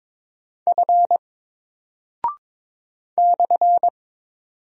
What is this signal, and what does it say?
Morse code, heard as a keyed tone, about 22 wpm, 700 Hz. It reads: F /
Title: Single Character (Letter:Number:Punctuation:Prosign) 22wpm